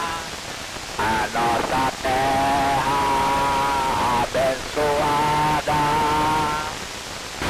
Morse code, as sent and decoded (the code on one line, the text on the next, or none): none